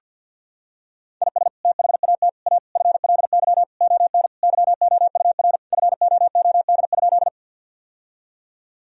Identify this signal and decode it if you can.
Morse code, heard as a keyed tone, about 50 wpm, 700 Hz. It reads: IS THAT A FLY ON YOUR FOOD?
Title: Sentences from Top 1000 Words - Review Part 2 - 50wpm